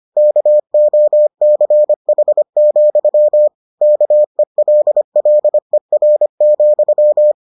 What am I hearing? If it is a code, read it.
Morse code, heard as a keyed tone, about 25 wpm, 600 Hz. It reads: KOCH, KELLER,